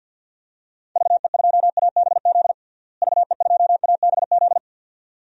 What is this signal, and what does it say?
Morse code, heard as a keyed tone, about 53 wpm, 700 Hz. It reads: VE2ABZ VE2ABZ
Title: DXCC-100 Call Signs - 6 Characters - Repeated 2x - 50wpm